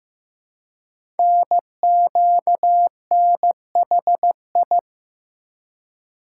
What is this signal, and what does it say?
Morse code, heard as a keyed tone, about 15 wpm, 700 Hz. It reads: NQNHI
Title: Code Groups - 5 Letters - 15wpm